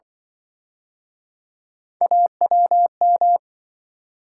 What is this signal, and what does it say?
Morse code, heard as a keyed tone, about 30 wpm, 700 Hz. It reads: AWM